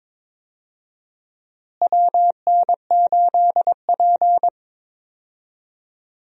Morse code, heard as a keyed tone, about 22 wpm, 700 Hz. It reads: WN8P
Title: USA Call Signs - 4 Characters - Repeated 3x - 22wpm